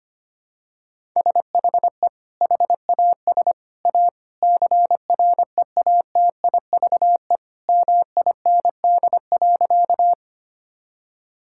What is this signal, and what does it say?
Morse code, heard as a keyed tone, about 25 wpm, 700 Hz. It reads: SHE HAS A CREATIVE MIND.
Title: Sentences from Top 1000 Words - Review Part 5 - 25wpm